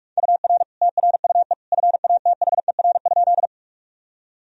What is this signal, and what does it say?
Morse code, heard as a keyed tone, about 47 wpm, 700 Hz. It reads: UR TRUE FATHER?
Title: Sentences from Top 400 Words and Abbreviations 45wpm